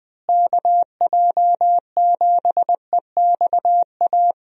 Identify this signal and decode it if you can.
Morse code, heard as a keyed tone, about 20 wpm, 700 Hz. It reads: KJ7EXA
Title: USA Call Signs - 6 Characters - Repeated 2x - 20wpm